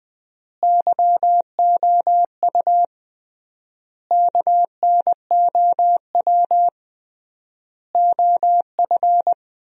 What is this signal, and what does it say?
Morse code, heard as a keyed tone, about 21 wpm, 700 Hz. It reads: YOU KNOW OF
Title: 3x Word-Spacing - Sentences from Top 100 Words - 20wpm